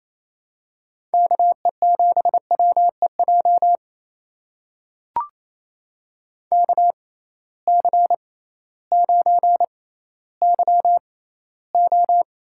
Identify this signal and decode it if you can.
Morse code, heard as a keyed tone, about 28 wpm, spaced as for 15 wpm, 700 Hz. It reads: KE7WEJ KC9YO
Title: USA Call Signs - 6 Characters - 6x Character Spacing - 28wpm